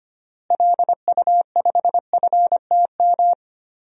Morse code, heard as a keyed tone, about 25 wpm, 700 Hz. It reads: LU5FTM
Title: DXCC-100 Call Signs - 6 Characters - Repeated 2x - 25wpm